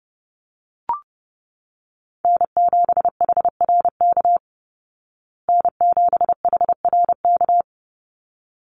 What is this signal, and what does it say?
Morse code, heard as a keyed tone, about 30 wpm, 700 Hz. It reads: N7HRK N7HRK
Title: Call Signs Repeated 3x - 30wpm